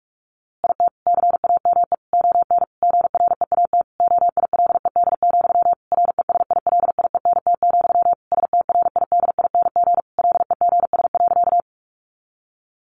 Morse code, heard as a keyed tone, about 44 wpm, 700 Hz. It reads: IT CAME ON GREAT OILED, RESILIENT, STRIDING LEGS.
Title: A Sound Of Thunder 45wpm